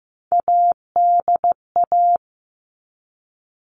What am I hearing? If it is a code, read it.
Morse code, heard as a keyed tone, about 15 wpm, 700 Hz. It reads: ADA
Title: All Cut Numbers - 3-Digit 15wpm